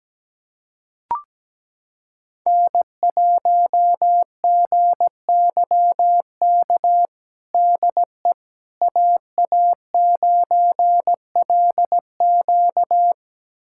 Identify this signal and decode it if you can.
Morse code, heard as a keyed tone, about 17 wpm, 700 Hz. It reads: N1GYK DE AA9LQ
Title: USA Callsign DE Callsign - 5 Characters - 17wpm